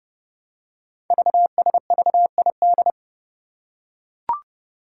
Morse code, heard as a keyed tone, about 30 wpm, 700 Hz. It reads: VSVID
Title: Code Groups - 5 Letters - 30wpm